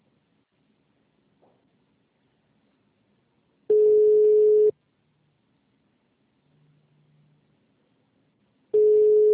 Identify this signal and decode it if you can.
Ringback tone (Europe, 425 Hz).